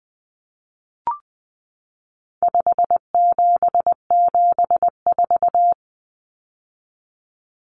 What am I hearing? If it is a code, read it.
Morse code, heard as a keyed tone, about 20 wpm, 700 Hz. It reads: 5774
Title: Common Cut Numbers - 4-Digit 20wpm